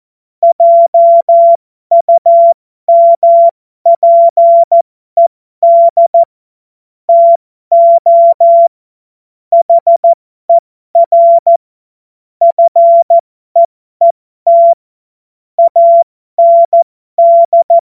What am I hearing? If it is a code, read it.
Morse code, heard as a keyed tone, about 14 wpm, spaced as for 12 wpm, 680 Hz. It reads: JUMPED TO HER FEET AND
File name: Morsekurs / Witze